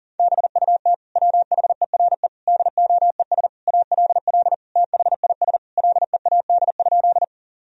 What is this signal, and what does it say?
Morse code, heard as a keyed tone, about 40 wpm, 700 Hz. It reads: BUT WHERE DOES ALL THIS LEAD?
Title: Sentences from Top 1000 Words - Review Part 2 - 40wpm